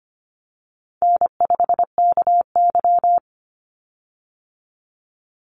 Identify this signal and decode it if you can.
Morse code, heard as a keyed tone, about 25 wpm, 700 Hz. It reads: N5KY
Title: Easy Call Signs - 25wpm